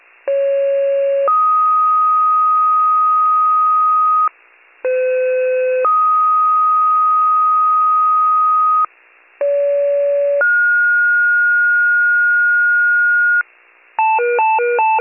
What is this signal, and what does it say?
Two-tone page, heard as a one-second tone, then a three-second tone, about 550 then 1250 Hz, about 510 then 1250 Hz, about 570 then 1430 Hz. It is heard from 0.3 to 13.4 s.